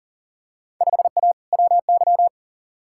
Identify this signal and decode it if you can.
Morse code, heard as a keyed tone, about 40 wpm, 700 Hz. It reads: 5A WY
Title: Field Day Exchanges Encore - 40wpm